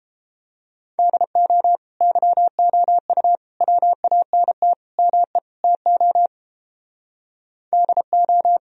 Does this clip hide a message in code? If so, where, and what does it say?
Yes: Morse code from 1.0 to 8.6 s, heard as a keyed tone, about 29 wpm, 700 Hz. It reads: DO YOU WANT ME TO DO